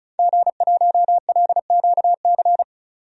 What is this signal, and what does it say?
Morse code, heard as a keyed tone, about 36 wpm, 700 Hz. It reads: G1LQC
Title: DXCC-100 Call Signs - 5 Characters - 35wpm